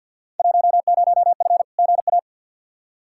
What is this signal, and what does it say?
Morse code, heard as a keyed tone, about 47 wpm, 700 Hz. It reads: 10F GA
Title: Field Day Exchanges Encore - 50wpm